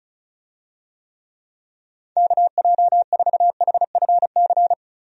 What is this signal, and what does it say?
Morse code, heard as a keyed tone, about 36 wpm, 700 Hz. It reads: KJ4HFC
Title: USA Call Signs - 6 Characters - Repeated 3x - 35wpm